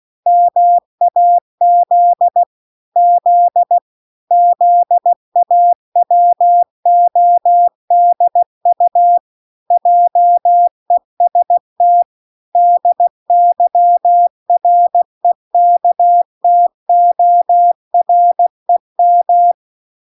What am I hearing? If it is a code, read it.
Morse code, heard as a keyed tone, about 16 wpm, 700 Hz. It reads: MAZ Z ZAWODU JEST DYREKTOREM